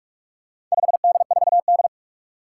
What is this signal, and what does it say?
Morse code, heard as a keyed tone, about 46 wpm, 700 Hz. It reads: 5D4D
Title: Four Challenging Letter-Number Combinations 45wpm